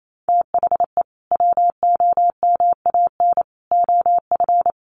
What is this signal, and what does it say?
Morse code, heard as a keyed tone, about 28 wpm, 700 Hz. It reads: THE WOMAN OF
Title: Sentences from Top 1500 Words 28wpm